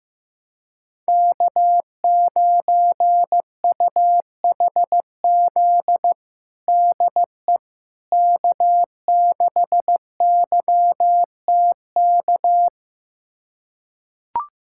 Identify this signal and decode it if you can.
Morse code, heard as a keyed tone, about 15 wpm, 700 Hz. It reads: K9UHZ DE K6YTK
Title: USA Callsign DE Callsign - 5 Characters - 15wpm